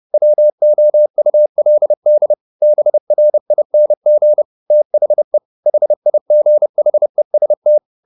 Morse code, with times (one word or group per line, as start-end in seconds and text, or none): 0.1-2.3 WOULD
2.6-4.4 BRING
4.7-5.4 THE
5.7-7.8 HIGHEST